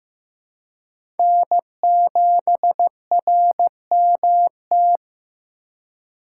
Morse code, heard as a keyed tone, about 15 wpm, 700 Hz. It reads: N7RMT
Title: USA Call Signs - 5 Characters - 3x Character Spacing - 15wpm